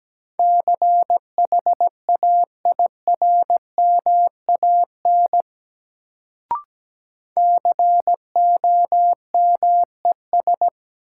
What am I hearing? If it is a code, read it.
Morse code, heard as a keyed tone, about 17 wpm, 700 Hz. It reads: CHAIRMAN COMES
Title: Sets of 3 Words - New Word Drill - 17wpm